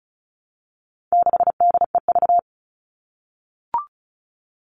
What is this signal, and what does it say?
Morse code, heard as a keyed tone, about 34 wpm, 700 Hz. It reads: 6DEV
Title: All Cut Numbers - 4-Digit 35wpm